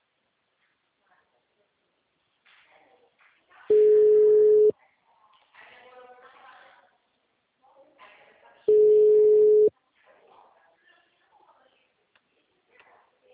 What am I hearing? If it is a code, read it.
Ringback tone (Europe, 425 Hz), with speech.
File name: out- -2017- - - .3929